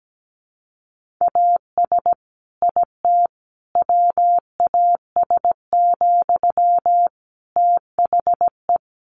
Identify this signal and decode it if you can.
Morse code, heard as a keyed tone, about 17 wpm, 700 Hz. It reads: AS IT WAS, THE
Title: Sentences from Top 1500 Words 17wpm